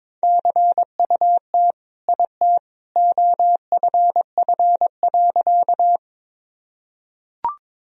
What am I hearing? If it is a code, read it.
Morse code, heard as a keyed tone, about 21 wpm, 700 Hz. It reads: CUT IT OFF.
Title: Sentences from Top 1000 Words - Review Part 1 - 22wpm